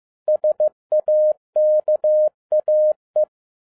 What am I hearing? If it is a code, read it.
Morse code, heard as a keyed tone, about 15 wpm, 610 Hz. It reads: SAKAE
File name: FIELD DAY CONTEST QRV FROM SAKAE VILLAGE 15wpm